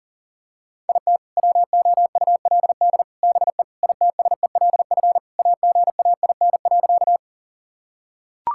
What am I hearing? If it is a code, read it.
Morse code, heard as a keyed tone, about 40 wpm, 700 Hz. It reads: IT WOULD BE ITSELF AGAIN.
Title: Sentences from Top 1000 Words - Review Part 5 - 40wpm